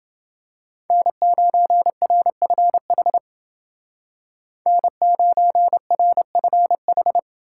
Morse code, heard as a keyed tone, about 28 wpm, 700 Hz. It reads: N9RFH N9RFH